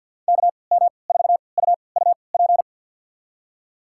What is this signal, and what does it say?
Morse code, heard as a keyed tone, about 51 wpm, spaced as for 28 wpm, 700 Hz. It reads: KM4UUP
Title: USA Call Signs - 3x Character Spacing - 50wpm